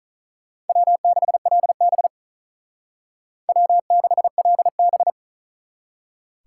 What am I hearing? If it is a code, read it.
Morse code, heard as a keyed tone, about 46 wpm, 700 Hz. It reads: W6LB W6LB